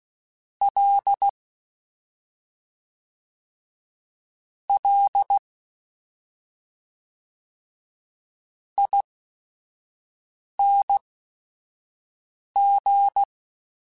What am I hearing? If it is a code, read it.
Morse code, heard as a keyed tone, about 16 wpm, spaced as for 5 wpm, 800 Hz. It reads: L L ING
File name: Special INGs / Special INGs-mp3s / SlowToFaster ing @5